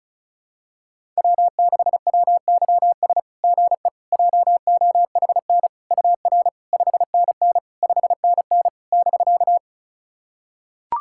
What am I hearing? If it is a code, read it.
Morse code, heard as a keyed tone, about 36 wpm, 690 Hz. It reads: W6WYS GE JOHN UR 5NN 5NN <BK>
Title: SOTA QSO Practice 35wpm